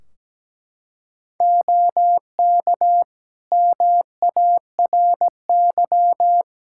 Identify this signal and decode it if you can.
Morse code, heard as a keyed tone, about 17 wpm, 700 Hz. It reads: OK MARY
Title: Practice QSO - Exchange 3 - More Info - 17wpm